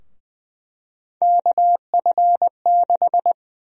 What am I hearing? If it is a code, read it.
Morse code, heard as a keyed tone, about 20 wpm, 700 Hz. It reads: KF6
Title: USA Call Sign Prefixes - Repeated 2x - 20wpm